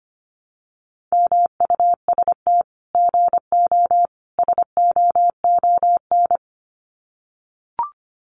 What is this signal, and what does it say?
Morse code, heard as a keyed tone, about 25 wpm, 700 Hz. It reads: MUST GO SOON
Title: Common QSO Phrases 25wpm